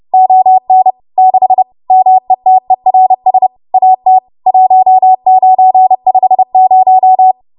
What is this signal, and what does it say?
Morse code, heard as a keyed tone, about 30 wpm, 750 Hz. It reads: ON 6 METERS AT 1950
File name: Morse / 30WPM